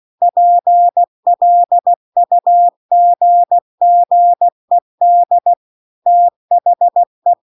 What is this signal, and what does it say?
Morse code, heard as a keyed tone, about 16 wpm, 700 Hz. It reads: PLUGGED THE